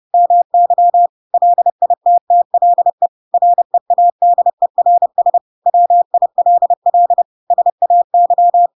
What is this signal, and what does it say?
Morse code, heard as a keyed tone, about 30 wpm, 700 Hz. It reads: MY LITTLE READERS WILL SAY
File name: Pinocchio / Pinocchio ch1 30 30 ik1hge